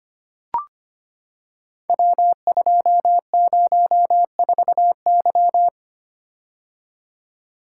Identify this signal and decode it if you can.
Morse code, heard as a keyed tone, about 25 wpm, 700 Hz. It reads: W204Y
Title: Code Groups - 5 Letter:Numbers - 25wpm